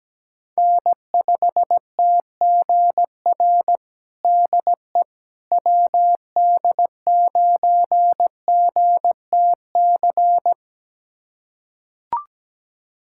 Morse code, heard as a keyed tone, about 17 wpm, 700 Hz. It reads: N5TGR DE WD9GTC
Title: USA Callsign DE Callsign 17wpm